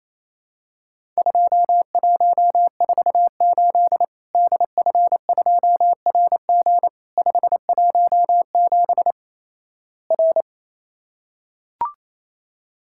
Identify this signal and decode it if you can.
Morse code, heard as a keyed tone, about 28 wpm, 700 Hz. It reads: 2148 DF2RG 517 R
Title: Worked All Europe DX Contest - QTC Exchange - 28wpm